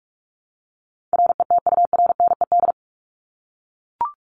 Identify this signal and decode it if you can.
Morse code, heard as a keyed tone, about 44 wpm, 700 Hz. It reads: RETURNED
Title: ED Suffix - Top 100 Words with Base Words 45wpm